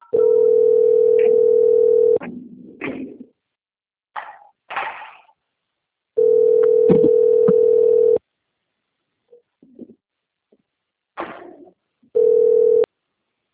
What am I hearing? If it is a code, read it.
Ringback tone (North America), 440 + 480 Hz.
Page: Duque de Caxias - RJ